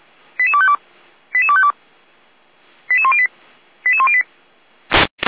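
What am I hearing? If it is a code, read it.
Five-tone selective call (ZVEI), heard as quick runs of five tones, tones of about 70 ms. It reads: 88252 88252 88108 88108
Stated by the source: ZVEI five-tone call 88247 received by SDR, from the feuerwehr-orlamuende.org fire brigade audio archive.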